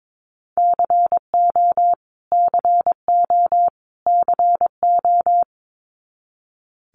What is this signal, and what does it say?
Morse code, heard as a keyed tone, about 22 wpm, 700 Hz. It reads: CO CO CO
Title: US States Repeated 3x 22wpm